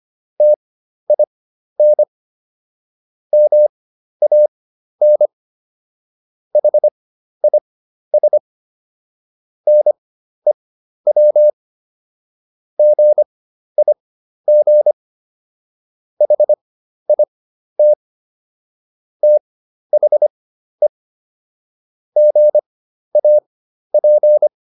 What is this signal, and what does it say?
Morse code, heard as a keyed tone, about 25 wpm, spaced as for 12 wpm, 600 Hz. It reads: TIN MAN HIS NEW GIG HIT THE GAP